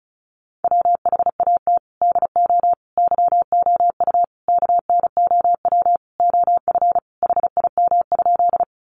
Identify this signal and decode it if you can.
Morse code, heard as a keyed tone, about 34 wpm, 700 Hz. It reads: WHAT DO YOU KNOW OF HIM?
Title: Sentences from Top 100 Words 35wpm